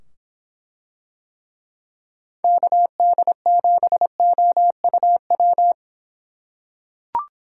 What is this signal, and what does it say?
Morse code, heard as a keyed tone, about 24 wpm, 700 Hz. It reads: KD7OUW